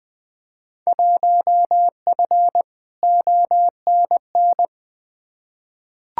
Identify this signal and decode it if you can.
Morse code, heard as a keyed tone, about 21 wpm, 700 Hz. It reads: 1F ONN
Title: Field Day Exchanges Encore - 20wpm